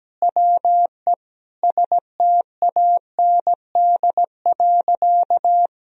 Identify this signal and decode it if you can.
Morse code, heard as a keyed tone, about 17 wpm, 700 Hz. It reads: WE STAND.
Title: Sentences from Top 1000 Words - Review Part 2 - 17wpm